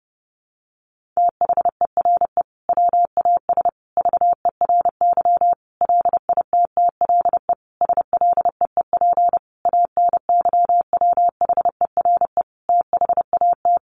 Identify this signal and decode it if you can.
Morse code, heard as a keyed tone, about 30 wpm, 700 Hz. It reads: THERE WAS VERY LITTLE SLEEP ANYWHERE THAT
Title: Sentences from Top 1400 Words 30wpm